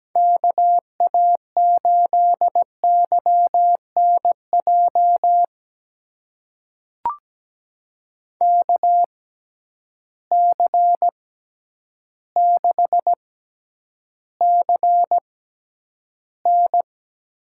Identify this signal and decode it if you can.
Morse code, heard as a keyed tone, about 17 wpm, spaced as for 9 wpm, 700 Hz. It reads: KA8YNJ KC6CN
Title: USA Call Signs - 6 Characters - 6x Character Spacing - 17wpm